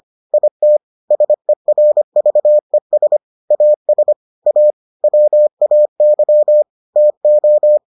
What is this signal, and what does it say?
Morse code, heard as a keyed tone, about 25 wpm, 600 Hz. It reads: IT SERVES AS A WAY TO